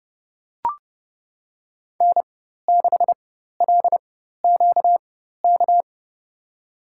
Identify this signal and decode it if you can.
Morse code, heard as a keyed tone, about 30 wpm, spaced as for 14 wpm, 700 Hz. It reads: N6LQK
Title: USA Call Signs - 5 Characters - 4x Character Spacing - 30wpm